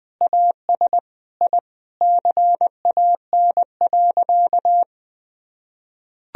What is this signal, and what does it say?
Morse code, heard as a keyed tone, about 20 wpm, 700 Hz. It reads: AS I CAN.
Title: Sentences from Top 1000 Words - Review Part 1 - 20wpm